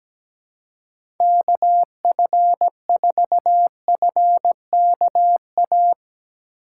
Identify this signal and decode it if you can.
Morse code, heard as a keyed tone, about 17 wpm, 700 Hz. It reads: KF4FKA
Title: USA Call Signs - 6 Characters - 3x Character Spacing - 17wpm